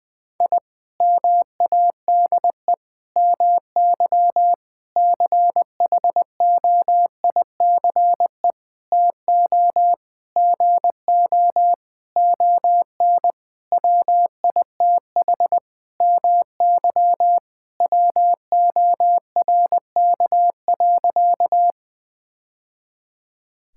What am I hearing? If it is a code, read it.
Morse code, heard as a keyed tone, about 21 wpm, 700 Hz. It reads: I MADE MY CHOICE TO GO ON WITH MY WORK.
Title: Sentences from Top 1000 Words - Review Part 6 - 20wpm